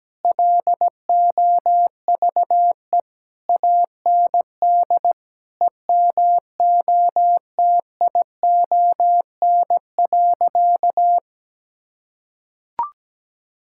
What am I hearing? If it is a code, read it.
Morse code, heard as a keyed tone, about 17 wpm, 700 Hz. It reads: LOVE AND EMOTION.
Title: Sentences from Top 1600 Words 17wpm